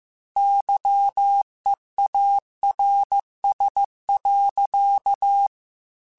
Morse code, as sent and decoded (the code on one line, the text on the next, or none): -.-- . .- .-. ... .-.-.-
YEARS.